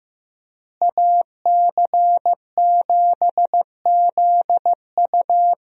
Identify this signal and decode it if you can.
Morse code, heard as a keyed tone, about 15 wpm, 700 Hz. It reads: AC7ZU
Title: USA Call Signs - 5 Characters - Repeated 3x - 15wpm